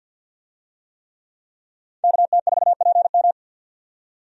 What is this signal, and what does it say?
Morse code, heard as a keyed tone, about 47 wpm, 700 Hz. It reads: KT4PM
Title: USA Call Signs - 5 Characters - 50wpm